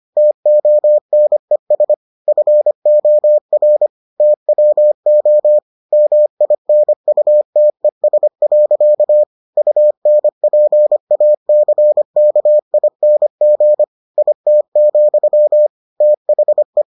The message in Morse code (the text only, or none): TONES FOR TWO MINUTES. UNPACKING IT, THE